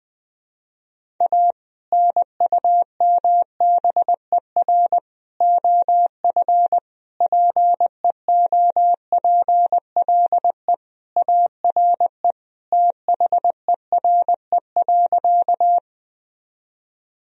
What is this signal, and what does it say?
Morse code, heard as a keyed tone, about 20 wpm, 700 Hz. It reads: A NUMBER OF PEOPLE ARE THERE.
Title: Sentences from Top 1000 Words - Review Part 1 - 20wpm